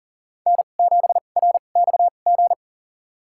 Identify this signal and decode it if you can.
Morse code, heard as a keyed tone, about 40 wpm, spaced as for 29 wpm, 700 Hz. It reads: N7RXG
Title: USA Call Signs - 5 Characters - 2x Character Spacing - 40wpm